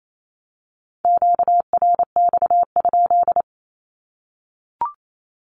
Morse code, heard as a keyed tone, about 28 wpm, 700 Hz. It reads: QRX?
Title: Q-Codes 28wpm